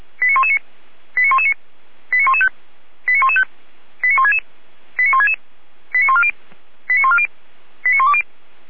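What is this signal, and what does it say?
Five-tone selective call (ZVEI), heard as quick runs of five tones, tones of about 80 ms. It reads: 89119 89119 89115 89115 89166 89166 89140 89140 89120
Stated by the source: ZVEI five-tone selective call 89092, received on SDR.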